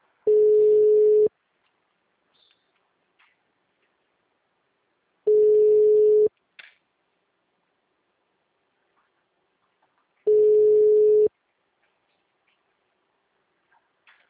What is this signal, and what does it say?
Ringback tone (Europe, 425 Hz).
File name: out- -2021- - - .2480